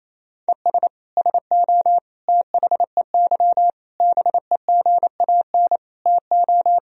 Morse code, heard as a keyed tone, about 28 wpm, 700 Hz. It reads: ES SO THEY BEGAN TO
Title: Sentences from Top 400 Words and Abbreviations 28wpm